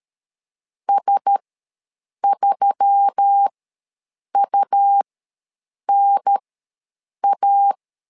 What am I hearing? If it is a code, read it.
Morse code, heard as a keyed tone, about 13 wpm, spaced as for 7 wpm, 800 Hz. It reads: S3UNA
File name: LEZIONI / 4 Q1S3 ws 12wpm 2038